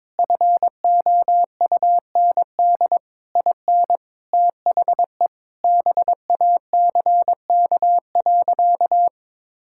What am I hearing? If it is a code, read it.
Morse code, heard as a keyed tone, about 22 wpm, 700 Hz. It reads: FOUND IN THE BACK.
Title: Sentences from Top 1000 Words - Review Part 5 - 22wpm